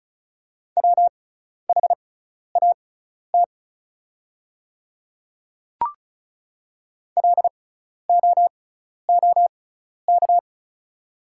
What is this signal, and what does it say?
Morse code, heard as a keyed tone, about 34 wpm, spaced as for 12 wpm, 700 Hz. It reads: WHAT LOOK
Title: 6x Character Spacing - Top 100 Words - 35wpm